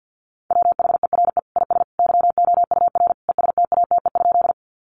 Morse code, heard as a keyed tone, about 50 wpm, 700 Hz. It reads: WHERE IS YOUR ESTATE?
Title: Sentences from Top 1200 Words 50wpm